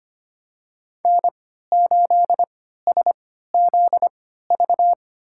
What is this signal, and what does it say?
Morse code, heard as a keyed tone, about 25 wpm, spaced as for 14 wpm, 700 Hz. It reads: N8SZV
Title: USA Call Signs - 5 Characters - 3x Character Spacing - 25wpm